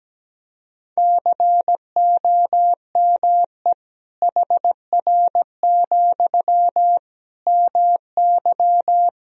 Morse code, heard as a keyed tone, about 17 wpm, 700 Hz. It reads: COME HR, MY
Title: Sentences from Top 300 Words and Abbreviations 17wpm